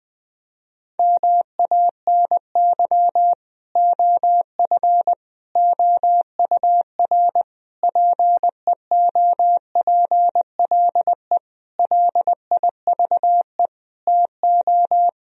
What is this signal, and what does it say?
Morse code, heard as a keyed tone, about 20 wpm, 700 Hz. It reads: MANY OF OUR PEOPLE LIVE TO
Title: Sentences from Top 1000 Words - Review Part 3 - 20wpm